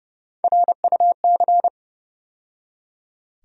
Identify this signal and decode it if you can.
Morse code, heard as a keyed tone, about 30 wpm, 700 Hz. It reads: RUC
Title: USA Call Sign Postfixes - 3 Letters 30wpm